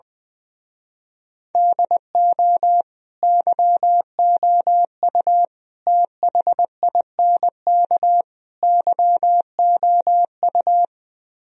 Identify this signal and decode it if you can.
Morse code, heard as a keyed tone, about 20 wpm, 700 Hz. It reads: DO YOU THINK YOU
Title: Speed-Racing - Sets of 4 Words 20wpm